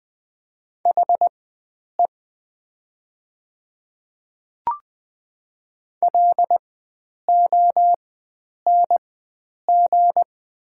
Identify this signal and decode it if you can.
Morse code, heard as a keyed tone, about 20 wpm, spaced as for 9 wpm, 700 Hz. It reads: HE LONG
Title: 4x Character Spacing - Top 300 Words - 20wpm